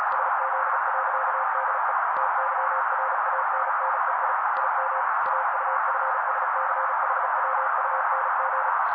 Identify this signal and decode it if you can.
Morse code, heard as a keyed tone, about 35 wpm, 530 Hz. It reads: SZ.DE TYRANNEI IM AUFSCHWUNG
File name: daily / 35wpm noise